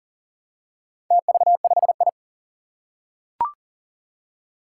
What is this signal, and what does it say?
Morse code, heard as a keyed tone, about 40 wpm, 700 Hz. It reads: TV5I
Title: Four Challenging Letter-Number Combinations 40wpm